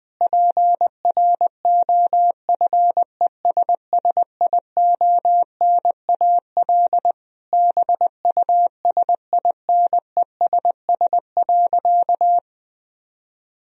Morse code, heard as a keyed tone, about 19 wpm, 700 Hz. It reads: PROFESSIONAL BUSINESS.
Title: Sentences from Top 1000 Words - Review Part 4 - 20wpm